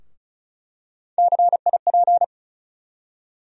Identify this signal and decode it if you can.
Morse code, heard as a keyed tone, about 36 wpm, 700 Hz. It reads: CIP
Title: USA Call Sign Postfixes - 3 Letters 35wpm